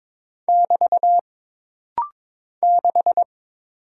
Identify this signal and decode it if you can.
Morse code, heard as a keyed tone, about 22 wpm, 700 Hz. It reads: <BT> 6
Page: This practice set is composed of Letters, Numbers, common punctuations, and common prosigns.